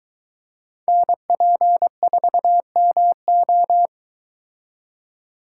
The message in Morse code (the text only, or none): NP4MO